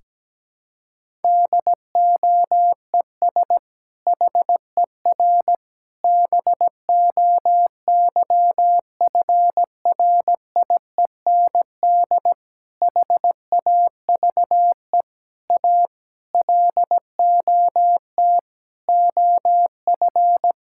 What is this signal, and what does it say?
Morse code, heard as a keyed tone, about 17 wpm, 700 Hz. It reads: DOES HER BOYFRIEND HAVE A LOT OF
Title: Sentences from Top 1000 Words - Review Part 5 - 17wpm